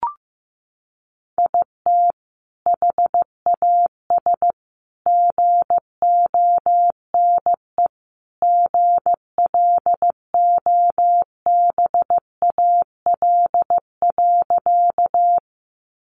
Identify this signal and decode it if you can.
Morse code, heard as a keyed tone, about 15 wpm, 700 Hz. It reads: IT HAS GONE GLOBAL.
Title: Sentences from Top 1200 Words 15wpm